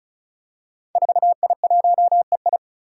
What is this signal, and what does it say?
Morse code, heard as a keyed tone, about 34 wpm, 700 Hz. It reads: 4I1EI
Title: Five Challenging Letter-Number Combinations 35wpm